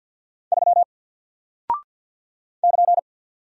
Morse code, heard as a keyed tone, about 48 wpm, 700 Hz. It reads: 3 <KN>
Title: Single Character (Letter:Number:Punctuation:Prosign) Alt 50wpm